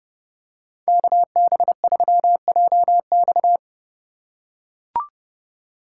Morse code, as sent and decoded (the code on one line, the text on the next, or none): -.- -... ...-- .--- -..-
KB3JX